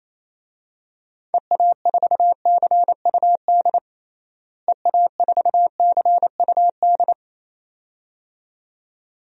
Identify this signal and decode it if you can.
Morse code, heard as a keyed tone, about 27 wpm, 700 Hz. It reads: EA4CUD EA4CUD
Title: DXCC-100 Call Signs - 6 Characters - Repeated 2x - 28wpm